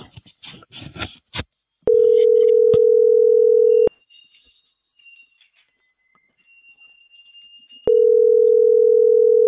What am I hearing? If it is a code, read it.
Ringback tone (North America), 440 + 480 Hz.